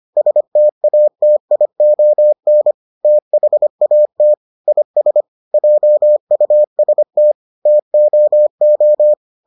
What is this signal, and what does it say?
Morse code, heard as a keyed tone, about 25 wpm, 600 Hz. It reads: STATION THAT IS JUST TOO